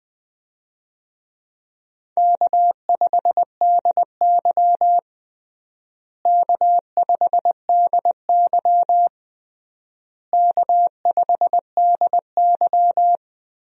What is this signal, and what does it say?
Morse code, heard as a keyed tone, about 20 wpm, 700 Hz. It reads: K5DY K5DY K5DY
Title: USA Call Signs - 4 Characters - Repeated 3x - 20wpm